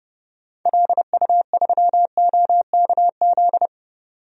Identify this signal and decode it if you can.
Morse code, heard as a keyed tone, about 32 wpm, 700 Hz. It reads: LU3OKZ
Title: DXCC-100 Call Signs - 6 Characters - 30wpm